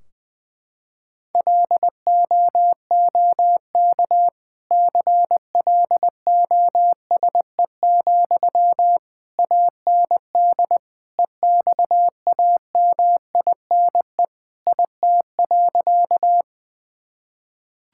Morse code, heard as a keyed tone, about 20 wpm, 700 Hz. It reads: LOOK CLOSE, AND EXAMINE IT.
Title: Sentences from Top 1600 Words 20wpm